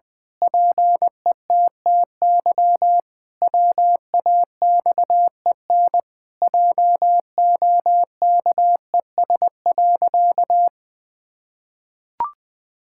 Morse code, heard as a keyed tone, about 20 wpm, 700 Hz. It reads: PETTY WAXEN JOKES.
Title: Pangram Challenge v2 - 20wpm